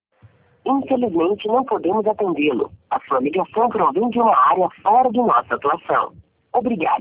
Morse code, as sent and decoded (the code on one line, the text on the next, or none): none